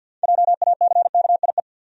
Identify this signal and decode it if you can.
Morse code, heard as a keyed tone, about 50 wpm, 700 Hz. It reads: JACKIE
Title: USA Common Names - 50wpm